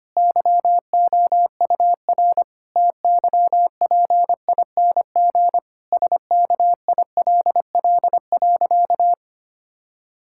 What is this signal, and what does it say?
Morse code, heard as a keyed tone, about 25 wpm, 700 Hz. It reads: YOUR TYPING SKILL.
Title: Pangram Challenge v2 - 25wpm